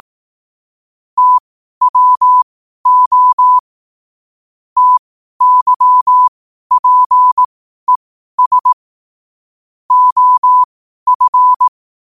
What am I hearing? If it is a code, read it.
Morse code, heard as a keyed tone, about 18 wpm, spaced as for 12 wpm, 1000 Hz. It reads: TWO TYPES OF
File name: 12wpm